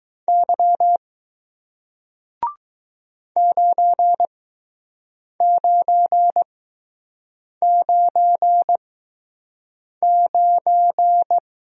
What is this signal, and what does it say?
Morse code, heard as a keyed tone, about 20 wpm, spaced as for 6 wpm, 700 Hz. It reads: Y 9999